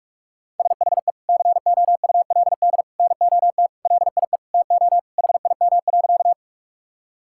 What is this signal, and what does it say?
Morse code, heard as a keyed tone, about 41 wpm, 700 Hz. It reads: SHE COULD NOT LIE TO HIM.
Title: Sentences from Top 1600 Words 45wpm